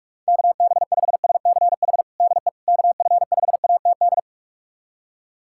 Morse code, heard as a keyed tone, about 46 wpm, 700 Hz. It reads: KB5SCH DE KF5ATD
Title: USA Callsign DE Callsign - 6 Characters - 45wpm